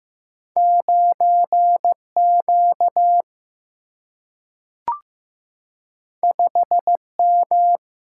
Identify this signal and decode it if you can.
Morse code, heard as a keyed tone, about 15 wpm, 700 Hz. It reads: 9 Q 5 M
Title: Single-Letter Single-Number 15wpm